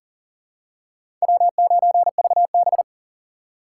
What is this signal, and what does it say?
Morse code, heard as a keyed tone, about 40 wpm, 700 Hz. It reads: W9VB
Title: USA Call Signs - 4 Characters - 2x Character Spacing - 40wpm